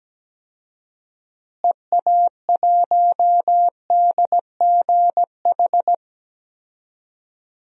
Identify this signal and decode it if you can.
Morse code, heard as a keyed tone, about 17 wpm, 700 Hz. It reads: EA1DGH
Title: DXCC-100 Call Signs - 6 Characters - Repeated 3x - 17wpm